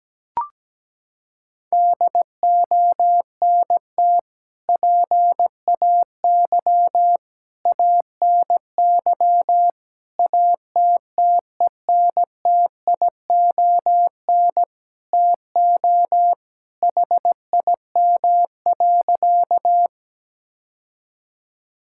Morse code, heard as a keyed tone, about 17 wpm, 700 Hz. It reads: DONT PAY ANY ATTENTION TO HIM.
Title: Sentences from Top 1000 Words - Review Part 6 - 17wpm